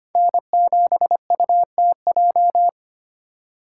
Morse code, heard as a keyed tone, about 32 wpm, 700 Hz. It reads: N7UTJ